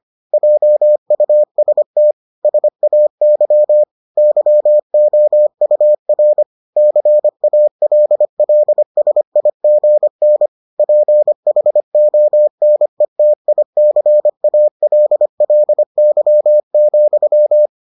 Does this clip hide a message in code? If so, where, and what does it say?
Yes: Morse code from 0.3 to 17.7 s, heard as a keyed tone, about 25 wpm, 600 Hz. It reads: JUST SAY YOUR CALLSIGN PHONETICALLY,